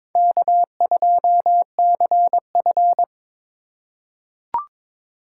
Morse code, heard as a keyed tone, about 22 wpm, 700 Hz. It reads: K2CF
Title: USA Call Signs - 4 Characters - 3x Character Spacing - 22wpm